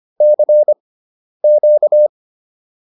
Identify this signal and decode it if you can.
Morse code, heard as a keyed tone, about 25 wpm, spaced as for 10 wpm, 600 Hz. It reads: CQ